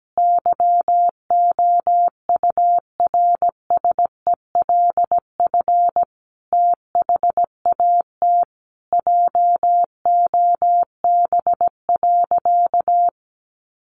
Morse code, heard as a keyed tone, about 17 wpm, 700 Hz. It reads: YOURSELF THAT JOB.
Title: Sentences from Top 1100 Words 17wpm